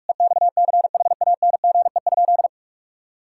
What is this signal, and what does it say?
Morse code, heard as a keyed tone, about 46 wpm, 700 Hz. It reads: EXCHANGE?
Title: Sentences from Top 1000 Words - Review Part 6 - 45wpm